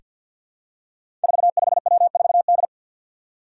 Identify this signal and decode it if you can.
Morse code, heard as a keyed tone, about 47 wpm, 700 Hz. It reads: 45W4D
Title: Five Challenging Letter-Number Combinations 50wpm